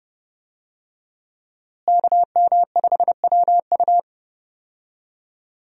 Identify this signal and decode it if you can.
Morse code, heard as a keyed tone, about 32 wpm, 700 Hz. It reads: KM5WU
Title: USA Call Signs - 5 Characters - Repeated 2x - 30wpm